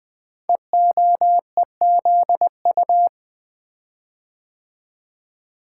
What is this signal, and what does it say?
Morse code, heard as a keyed tone, about 20 wpm, 700 Hz. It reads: EOEZU
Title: Code Groups - 5 Letters - 20wpm